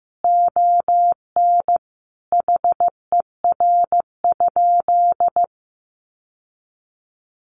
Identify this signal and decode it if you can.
Morse code, heard as a keyed tone, about 15 wpm, 700 Hz. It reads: ON HER?
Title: Sentences from Top 1200 Words 15wpm